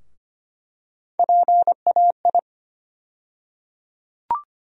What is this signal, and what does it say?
Morse code, heard as a keyed tone, about 25 wpm, 700 Hz. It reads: PAI
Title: USA Call Sign Postfixes - 3 Letters 25wpm